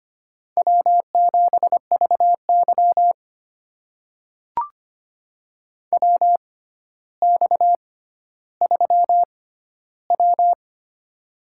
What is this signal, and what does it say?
Morse code, heard as a keyed tone, about 25 wpm, spaced as for 13 wpm, 700 Hz. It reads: W7VY WX3W
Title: USA Call Signs - 4 Characters - 6x Character Spacing - 25wpm